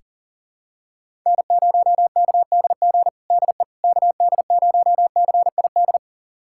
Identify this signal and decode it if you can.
Morse code, heard as a keyed tone, about 40 wpm, 700 Hz. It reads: N0KDG DE KD0CID
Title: USA Callsign DE Callsign 40wpm